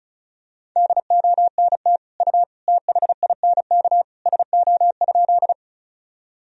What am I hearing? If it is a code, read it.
Morse code, heard as a keyed tone, about 36 wpm, 700 Hz. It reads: DONT U THINK SO?
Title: Sentences from Top 300 Words and Abbreviations 35wpm